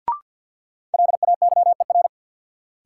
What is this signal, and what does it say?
Morse code, heard as a keyed tone, about 53 wpm, 700 Hz. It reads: LAYER
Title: Sentences from Top 1600 Words - New Word Drill Alt - 50wpm